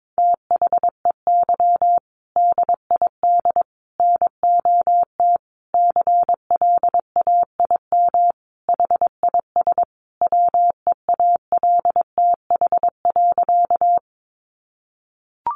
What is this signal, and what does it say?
Morse code, heard as a keyed tone, about 22 wpm, 700 Hz. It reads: THEY DID NOT CLAIM HIS WEALTH.
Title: Sentences from Top 1500 Words 22wpm